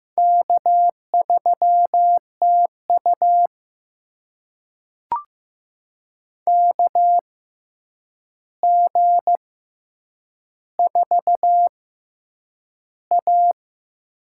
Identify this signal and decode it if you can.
Morse code, heard as a keyed tone, about 15 wpm, spaced as for 8 wpm, 700 Hz. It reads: K3TU KG4A
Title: USA Call Signs - 4 Characters - 6x Character Spacing - 15wpm